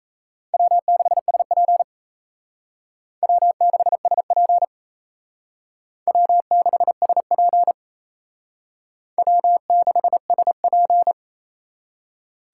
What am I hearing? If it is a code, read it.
Morse code, heard as a keyed tone, about 35 wpm, 700 Hz. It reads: W6SP W6SP W6SP W6SP
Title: Speed Racing - USA 1x2 Call Signs 28wpm